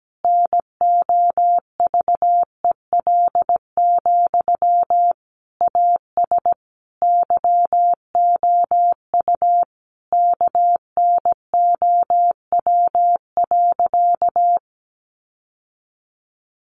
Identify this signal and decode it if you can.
Morse code, heard as a keyed tone, about 17 wpm, 700 Hz. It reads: NOVEL, AS YOU KNOW.
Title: Sentences from Top 1300 Words 17wpm